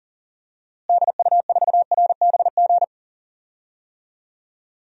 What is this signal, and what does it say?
Morse code, heard as a keyed tone, about 40 wpm, 700 Hz. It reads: DU4RBG
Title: DXCC-100 Call Signs - 6 Characters - Repeated 3x - 40wpm